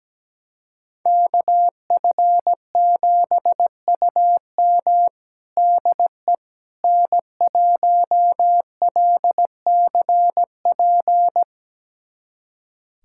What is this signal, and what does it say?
Morse code, heard as a keyed tone, about 17 wpm, 700 Hz. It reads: KF7UM DE N1LCP
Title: USA Callsign DE Callsign - 5 Characters - 17wpm